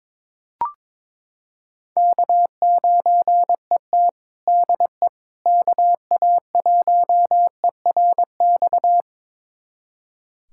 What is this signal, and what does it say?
Morse code, heard as a keyed tone, about 22 wpm, 700 Hz. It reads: K9ET DE KA1ERX
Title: USA Callsign DE Callsign 22wpm